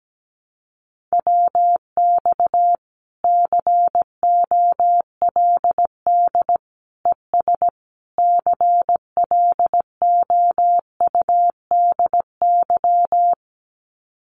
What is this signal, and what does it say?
Morse code, heard as a keyed tone, about 17 wpm, 700 Hz. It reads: WX COLD ES CLOUDY
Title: Common QSO Phrases 17wpm